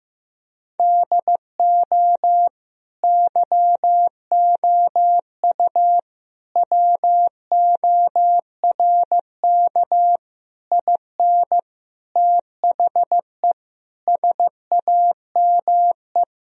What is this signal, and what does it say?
Morse code, heard as a keyed tone, about 16 wpm, 700 Hz. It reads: DO YOU WORK IN THE SAME
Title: Sentences from Top 1000 Words - Review Part 4 - 15wpm